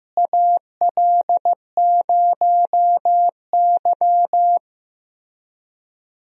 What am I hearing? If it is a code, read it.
Morse code, heard as a keyed tone, about 16 wpm, 700 Hz. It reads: AL0Y
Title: USA Call Signs - 4 Characters - Repeated 3x - 15wpm